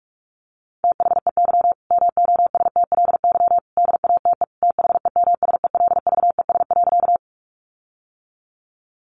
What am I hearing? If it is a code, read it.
Morse code, heard as a keyed tone, about 44 wpm, 700 Hz. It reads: THEY MOSTLY DATE THEMSELVES.
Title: Sentences from Top 1300 Words 45wpm